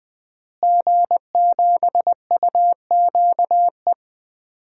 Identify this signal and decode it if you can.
Morse code, heard as a keyed tone, about 20 wpm, 700 Hz. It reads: G7UQE
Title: DXCC-100 Call Signs - 5 Characters - Repeated 2x - 20wpm